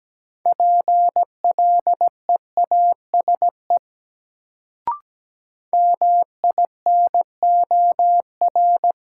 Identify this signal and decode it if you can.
Morse code, heard as a keyed tone, about 17 wpm, 700 Hz. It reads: PLEASE MINOR
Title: Sentences from Top 1500 Words - New Word Drill - 17wpm